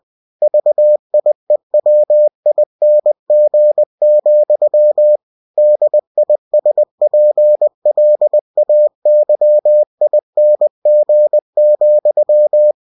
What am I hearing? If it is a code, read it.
Morse code, heard as a keyed tone, about 20 wpm, 600 Hz. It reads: VIEWING, DISPLAYING,
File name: cw-books / winnie-the-pooh / winnie-20wpm-page-0116